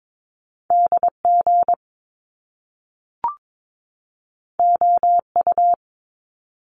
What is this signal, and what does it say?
Morse code, heard as a keyed tone, about 22 wpm, 700 Hz. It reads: DG OU
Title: Top Two Letter Combinations in Words 22wpm